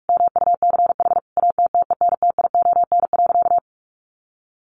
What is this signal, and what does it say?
Morse code, heard as a keyed tone, about 44 wpm, 700 Hz. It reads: MUCH ATTENTION.
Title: Sentences from Top 1000 Words 45wpm